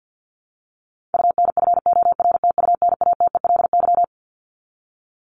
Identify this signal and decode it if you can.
Morse code, heard as a keyed tone, about 49 wpm, 700 Hz. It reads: UNFORTUNATELY
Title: LY Suffix - Top 100 Words with Base Words 50wpm